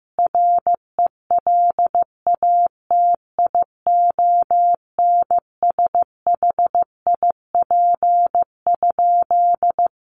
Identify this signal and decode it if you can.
Morse code, heard as a keyed tone, about 15 wpm, 700 Hz. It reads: RELATIONSHIP?
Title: Sentences from Top 1100 Words 15wpm